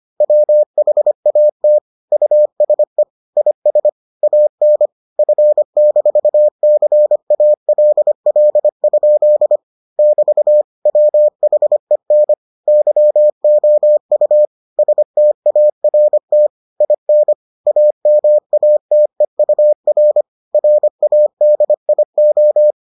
Morse code, heard as a keyed tone, about 25 wpm, 600 Hz. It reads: WHAT USE IS AN F-CALL? <BT> WHEN YOU START IN AMATEUR RADIO